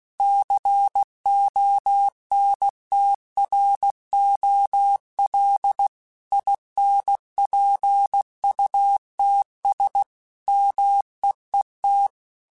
Morse code, heard as a keyed tone, about 16 wpm, 800 Hz. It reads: CONTROL INPUTS MEET
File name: Technical Subjects / Technical Subjects-mp3s / OR Gate @15